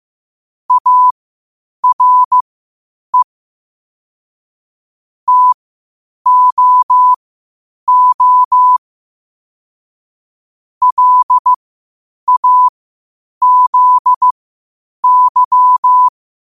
Morse code, heard as a keyed tone, about 15 wpm, spaced as for 8 wpm, 1000 Hz. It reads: ARE TOO LAZY
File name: qotdfiles / 8wpm